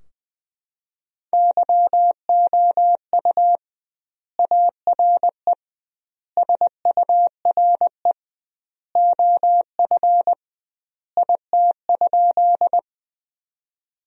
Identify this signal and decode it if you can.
Morse code, heard as a keyed tone, about 20 wpm, 700 Hz. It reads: YOU ARE SURE OF IT?
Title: 2x Word-Spacing - Sentences from Top 300 Words - 20wpm